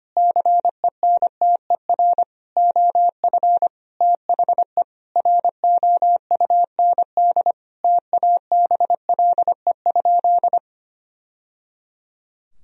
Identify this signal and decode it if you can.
Morse code, heard as a keyed tone, about 25 wpm, 700 Hz. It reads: CENTER OF THE ROUND TABLE?
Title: Sentences from Top 1000 Words - Review Part 2 - 25wpm